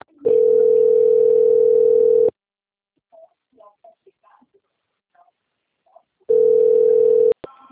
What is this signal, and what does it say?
Ringback tone (North America), 440 + 480 Hz.
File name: out- - - - - .12144